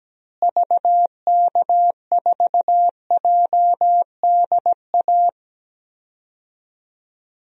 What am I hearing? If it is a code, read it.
Morse code, heard as a keyed tone, about 17 wpm, 700 Hz. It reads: VK4JDA
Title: DXCC-100 Call Signs - 6 Characters - Repeated 3x - 17wpm